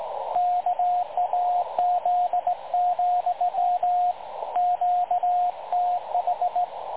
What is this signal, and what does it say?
Morse code, heard as a keyed tone, about 18 wpm, 720 Hz, with radio noise. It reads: KAZ, QTH
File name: dlazaawansowanych / nagraniazpasma / websdr recording 3565.0kHz